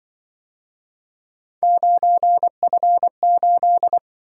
Morse code, heard as a keyed tone, about 25 wpm, 700 Hz. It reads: 9F8
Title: Speed-Racing - Three Letter-Number Combinations - 20wpm